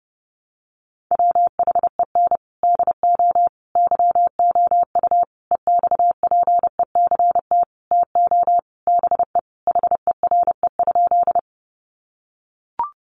Morse code, heard as a keyed tone, about 30 wpm, 700 Hz. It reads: WHEN DO YOU EXPECT TO BE HERE?
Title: Sentences from Top 1100 Words 30wpm